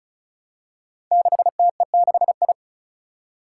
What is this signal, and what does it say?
Morse code, heard as a keyed tone, about 36 wpm, 700 Hz. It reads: 6TE6I
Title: Five Challenging Letter-Number Combinations 35wpm